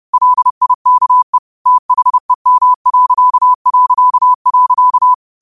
Morse code, heard as a keyed tone, about 30 wpm, 1000 Hz. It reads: LIKE THEM...